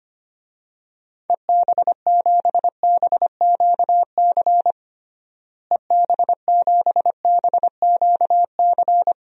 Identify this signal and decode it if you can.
Morse code, heard as a keyed tone, about 25 wpm, 700 Hz. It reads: EB7BQC EB7BQC
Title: DXCC-100 Call Signs - 6 Characters - Repeated 2x - 25wpm